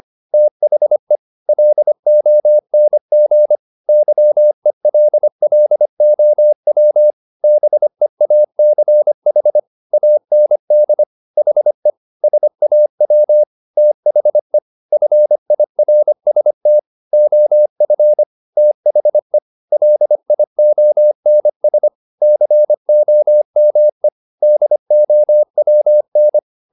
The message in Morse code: THE LONG YELLOW BEACH AND HE SAW THE FIRST OF THE LIONS COME DOWN

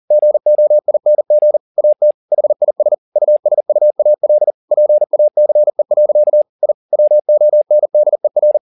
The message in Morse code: GOING AT HIS USUAL PACE. I WONDER

